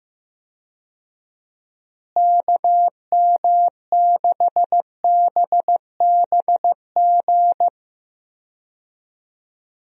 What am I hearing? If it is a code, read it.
Morse code, heard as a keyed tone, about 15 wpm, 700 Hz. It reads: KM6BBG
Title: USA Call Signs - 6 Characters - Repeated 2x - 15wpm